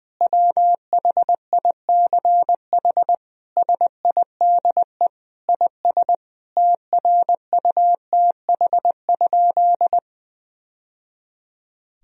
Morse code, heard as a keyed tone, about 20 wpm, 700 Hz. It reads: WHICH SIDE IS TRUTH?
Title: Sentences from Top 1000 Words - Review Part 1 - 20wpm